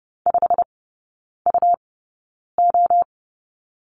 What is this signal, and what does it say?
Morse code, heard as a keyed tone, about 30 wpm, spaced as for 9 wpm, 700 Hz. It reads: 5UO
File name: Lesson 17e - Letter u - Three Characters 30wpm